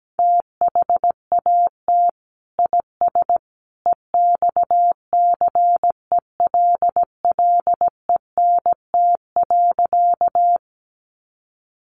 Morse code, heard as a keyed tone, about 17 wpm, 700 Hz. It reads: THAT IS EXCELLENT.
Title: Sentences from Top 1000 Words 17wpm